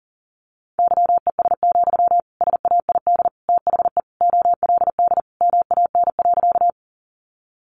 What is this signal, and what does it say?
Morse code, heard as a keyed tone, about 40 wpm, 700 Hz. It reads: YES, SAID THE OLD MAN.
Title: Sentences from Top 500 Words 40wpm